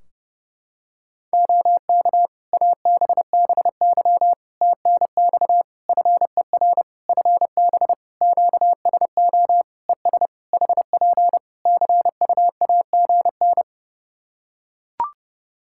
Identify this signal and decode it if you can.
Morse code, heard as a keyed tone, about 30 wpm, 700 Hz. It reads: OK ABBY TNX FER FB QSO ES HP CUAGN
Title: Practice QSO - Exchange 4 - Ending - 30wpm